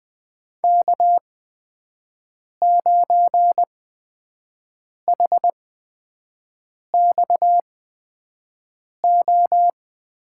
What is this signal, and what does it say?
Morse code, heard as a keyed tone, about 20 wpm, spaced as for 5 wpm, 700 Hz. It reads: K9HXO
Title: USA Call Signs - 5 Characters - 8x Character Spacing - 20wpm